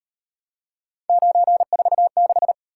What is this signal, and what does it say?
Morse code, heard as a keyed tone, about 37 wpm, 700 Hz. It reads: 946